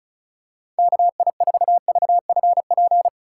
Morse code, heard as a keyed tone, about 34 wpm, 700 Hz. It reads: KI4VFP
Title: USA Call Signs - 2x Character Spacing - 35wpm